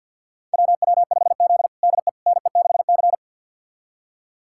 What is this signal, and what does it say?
Morse code, heard as a keyed tone, about 53 wpm, 700 Hz. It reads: WW5Z DE NE6C
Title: USA Callsign DE Callsign - 4 Characters - 50wpm